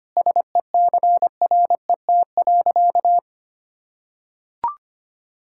Morse code, heard as a keyed tone, about 25 wpm, 700 Hz. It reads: SECRET.
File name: practice / 25wpm / Sentences with 4-Word Phrases - 25wpm